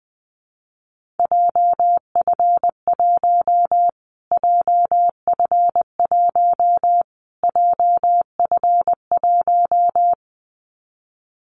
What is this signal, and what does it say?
Morse code, heard as a keyed tone, about 20 wpm, 700 Hz. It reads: JF1 JF1 JF1
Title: Callsign Prefixes for Top 25 DXCC - Repeated 3x - 20wpm